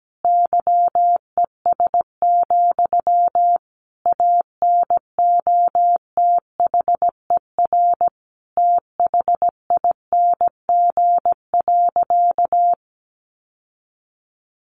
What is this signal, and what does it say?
Morse code, heard as a keyed tone, about 17 wpm, 700 Hz. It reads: YES, ANOTHER THING.
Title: Sentences from Top 600 Words 17wpm